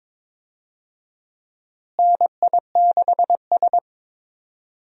Morse code, heard as a keyed tone, about 21 wpm, 700 Hz. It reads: NI6S
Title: USA Call Signs - 4 Characters - Repeated 3x - 22wpm